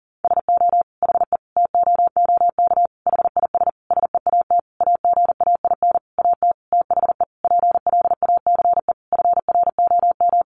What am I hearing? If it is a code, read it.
Morse code, heard as a keyed tone, about 40 wpm, 700 Hz. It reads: SO HE TOOK HIS SEAT AGAIN AT THE PLACE FROM
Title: Sentences from Top 1500 Words 40wpm